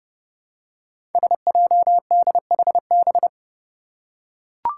Morse code, heard as a keyed tone, about 30 wpm, 700 Hz. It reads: SJDHB
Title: Five Challenging Letter-Number Combinations 30wpm